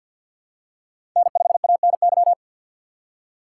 Morse code, heard as a keyed tone, about 51 wpm, 700 Hz. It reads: N5ANY
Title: USA Call Signs - 3x Character Spacing - 50wpm